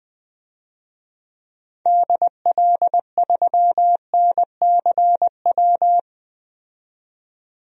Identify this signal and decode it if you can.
Morse code, heard as a keyed tone, about 20 wpm, 700 Hz. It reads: DL3NCW
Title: DXCC-100 Call Signs - 6 Characters - 20wpm